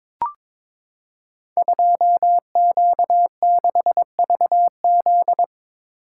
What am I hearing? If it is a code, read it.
Morse code, heard as a keyed tone, about 22 wpm, 700 Hz. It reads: 2Q6VZ
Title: Code Groups - 5 Letter:Numbers - 22wpm